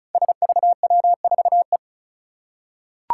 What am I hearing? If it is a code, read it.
Morse code, heard as a keyed tone, about 36 wpm, 700 Hz. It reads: SVW4E
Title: Five Challenging Letter-Number Combinations 35wpm